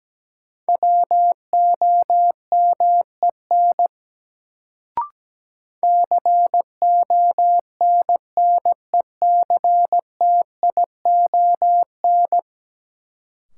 Morse code, heard as a keyed tone, about 17 wpm, 700 Hz. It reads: WOMEN CONNECTION
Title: Sentences from Top 1300 Words - New Word Drill - 17wpm